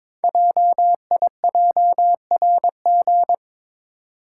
Morse code, heard as a keyed tone, about 23 wpm, 700 Hz. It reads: JIJRG
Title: Code Groups - 5 Letters - 22wpm